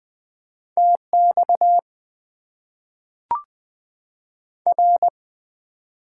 Morse code, heard as a keyed tone, about 20 wpm, 700 Hz. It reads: TX R
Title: Top 90 QSO Elements - 8x Character Spacing - 20wpm